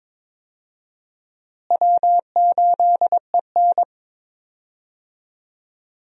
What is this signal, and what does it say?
Morse code, heard as a keyed tone, about 22 wpm, 700 Hz. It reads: W8EN
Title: USA Call Signs - 4 Characters - Repeated 3x - 22wpm